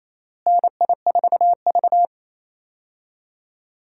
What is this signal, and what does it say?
Morse code, heard as a keyed tone, about 28 wpm, 700 Hz. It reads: NI4V
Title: USA Call Signs - 4 Characters - Repeated 2x - 28wpm